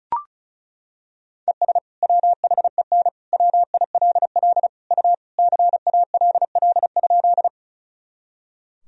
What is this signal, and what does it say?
Morse code, heard as a keyed tone, about 34 wpm, 700 Hz. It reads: ES WHEN WILL U CALL?
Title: Sentences from Top 100 Words and Abbreviations 35wpm